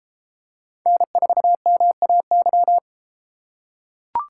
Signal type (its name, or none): Morse code